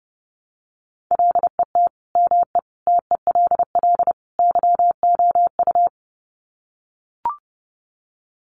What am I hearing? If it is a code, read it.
Morse code, heard as a keyed tone, about 30 wpm, 700 Hz. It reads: LET ME TELL YOU
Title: Sets of 4 Words 30wpm